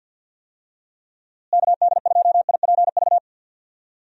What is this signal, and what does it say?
Morse code, heard as a keyed tone, about 51 wpm, 700 Hz. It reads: KD2IPV
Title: USA Call Signs - 6 Characters - Repeated 3x - 50wpm